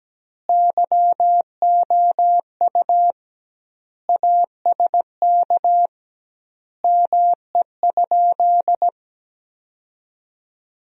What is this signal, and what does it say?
Morse code, heard as a keyed tone, about 17 wpm, 700 Hz. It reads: YOU ASK ME?
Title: 2x Word-Spacing - Sentences from Top 200 Words - 17wpm